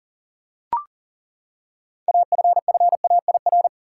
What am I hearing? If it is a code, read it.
Morse code, heard as a keyed tone, about 40 wpm, 700 Hz. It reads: AFFAIR
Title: Six Letter Words - 40wpm